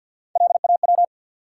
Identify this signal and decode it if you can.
Morse code, heard as a keyed tone, about 53 wpm, 700 Hz. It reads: LAW
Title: Three Letter Words - 50wpm